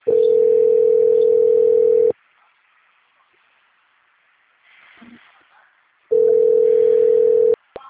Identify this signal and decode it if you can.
Ringback tone (North America), 440 + 480 Hz.
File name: out- - - - - .36801